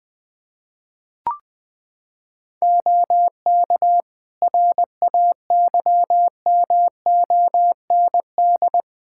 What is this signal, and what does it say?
Morse code, heard as a keyed tone, about 20 wpm, 700 Hz. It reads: OK RAYMOND
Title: Practice QSO - Exchange 3 - More Info - 20wpm